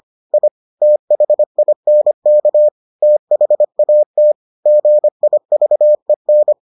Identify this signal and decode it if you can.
Morse code, heard as a keyed tone, about 25 wpm, 600 Hz. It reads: I THINK THAT GIVEN